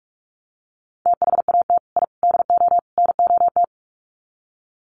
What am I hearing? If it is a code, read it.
Morse code, heard as a keyed tone, about 44 wpm, 700 Hz. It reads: THAT I DO NOT
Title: Sets of 4 Words 45wpm